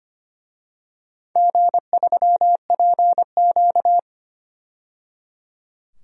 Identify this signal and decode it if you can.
Morse code, heard as a keyed tone, about 25 wpm, 700 Hz. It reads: G3PQ
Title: DXCC-100 Call Signs - 4 Characters - 25wpm